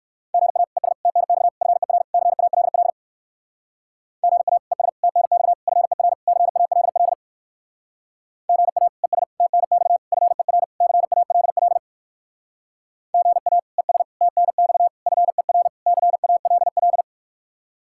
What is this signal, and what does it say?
Morse code, heard as a keyed tone, about 59 wpm, 700 Hz. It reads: GA ES TNX FER CALL GA ES TNX FER CALL GA ES TNX FER CALL GA ES TNX FER CALL